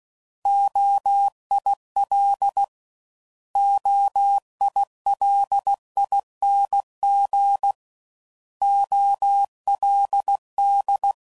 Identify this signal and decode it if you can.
Morse code, heard as a keyed tone, about 16 wpm, 800 Hz. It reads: OIL OILING OLD
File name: Prefixes / By First Letter / Os